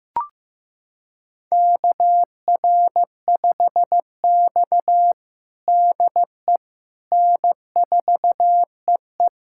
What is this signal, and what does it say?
Morse code, heard as a keyed tone, about 15 wpm, 700 Hz. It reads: KR5X DE N4EE
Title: USA Callsign DE Callsign - 4 Characters - 15wpm